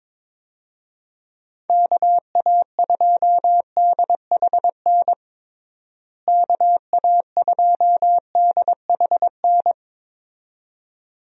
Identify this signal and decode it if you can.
Morse code, heard as a keyed tone, about 22 wpm, 700 Hz. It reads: KA2DHN KA2DHN
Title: USA Call Signs - 6 Characters - Repeated 2x - 22wpm